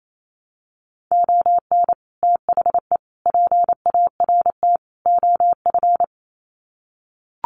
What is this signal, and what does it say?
Morse code, heard as a keyed tone, about 28 wpm, 700 Hz. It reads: ON THE PART OF
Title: Sets of 4 Words 28wpm